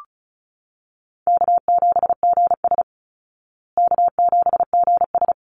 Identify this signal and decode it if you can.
Morse code, heard as a keyed tone, about 34 wpm, 700 Hz. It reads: K7GS K7GS
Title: Easy Call Signs Repeated 2x - 35wpm